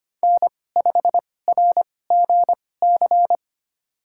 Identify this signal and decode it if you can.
Morse code, heard as a keyed tone, about 25 wpm, spaced as for 18 wpm, 700 Hz. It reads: N5RGC
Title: USA Call Signs - 5 Characters - 2x Character Spacing - 25wpm